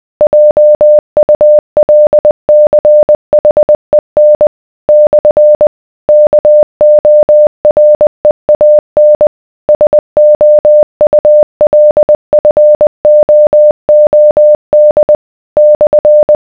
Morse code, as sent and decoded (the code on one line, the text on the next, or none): .--- ..- .-.. -.-. .... -. / -..-. / -.- --- .-. .- -. / ... --- ..- .-.. ..-. --- --- -.. / -..-.
JULCHEN / KOREAN SOULFOOD /